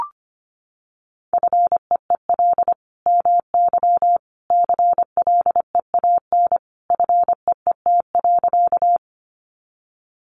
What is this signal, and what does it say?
Morse code, heard as a keyed tone, about 25 wpm, 700 Hz. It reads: FEEL MY CLEAN FEET.
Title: Sentences from Top 1100 Words 25wpm